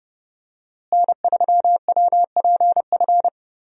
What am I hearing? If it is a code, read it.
Morse code, heard as a keyed tone, about 32 wpm, 700 Hz. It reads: N3WPF
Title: USA Call Signs - 5 Characters - Repeated 3x - 30wpm